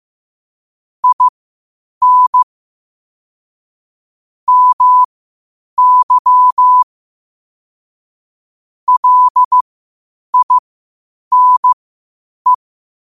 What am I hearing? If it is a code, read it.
Morse code, heard as a keyed tone, about 15 wpm, spaced as for 8 wpm, 1000 Hz. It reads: IN MY LINE
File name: qotdfiles / 8wpm